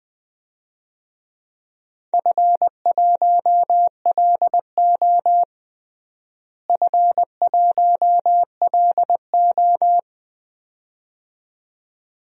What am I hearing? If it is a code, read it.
Morse code, heard as a keyed tone, about 20 wpm, 700 Hz. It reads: F1LO F1LO
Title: DXCC-100 Call Signs - 4 Characters - Repeated 2x - 20wpm